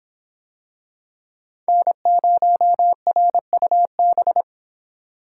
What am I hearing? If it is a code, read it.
Morse code, heard as a keyed tone, about 25 wpm, 700 Hz. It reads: N0RUB